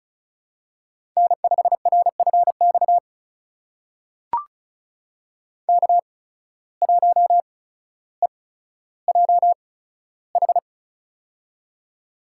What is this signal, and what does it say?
Morse code, heard as a keyed tone, about 36 wpm, spaced as for 15 wpm, 700 Hz. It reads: N5RFX K1EJH
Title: USA Call Signs - 5 Characters - 8x Character Spacing - 35wpm